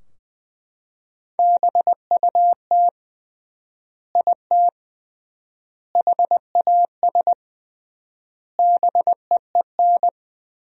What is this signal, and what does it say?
Morse code, heard as a keyed tone, about 21 wpm, 700 Hz. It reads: BUT IT HAS BEEN
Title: 3x Word-Spacing - Sentences from Top 100 Words - 20wpm